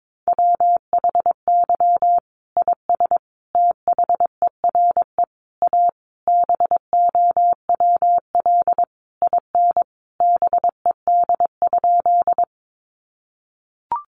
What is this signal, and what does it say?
Morse code, heard as a keyed tone, about 22 wpm, 700 Hz. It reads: WHY IS THERE A BOWL IN BED?
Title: Sentences from Top 1200 Words 22wpm